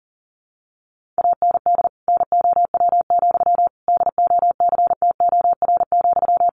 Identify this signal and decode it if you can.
Morse code, heard as a keyed tone, about 40 wpm, 700 Hz. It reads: AND NOW, DOCTOR,
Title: Sentences from Top 1500 Words 40wpm